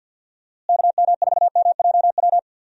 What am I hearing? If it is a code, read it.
Morse code, heard as a keyed tone, about 46 wpm, 700 Hz. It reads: KM4MJW
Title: USA Call Signs - 6 Characters - Repeated 3x - 50wpm